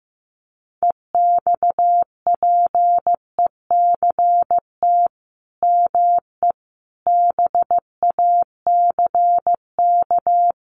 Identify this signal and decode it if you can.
Morse code, heard as a keyed tone, about 15 wpm, 700 Hz. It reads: EXPECT ME BACK
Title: Sentences from Top 1100 Words 15wpm